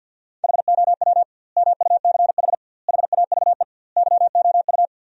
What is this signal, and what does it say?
Morse code, heard as a keyed tone, about 50 wpm, 700 Hz. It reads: HOW MUCH HAVE YOU
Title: 2x Word-Spacing - Sentences from Top 300 Words - 50wpm